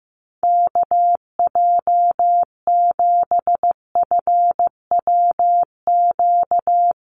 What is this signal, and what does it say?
Morse code, heard as a keyed tone, about 14 wpm, 700 Hz. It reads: KJ7FWQ
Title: USA Call Signs Repeated 2x 15wpm